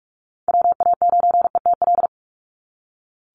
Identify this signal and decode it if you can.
Morse code, heard as a keyed tone, about 44 wpm, 700 Hz. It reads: WA9ETL
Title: USA Call Signs 45wpm